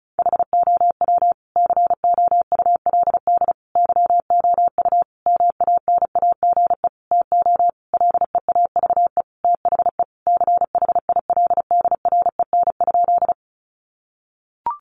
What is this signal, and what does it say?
Morse code, heard as a keyed tone, about 34 wpm, 700 Hz. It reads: HOW COULD YOU MANAGE TO LEAVE THE CHILDREN?
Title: Sentences from Top 1400 Words 35wpm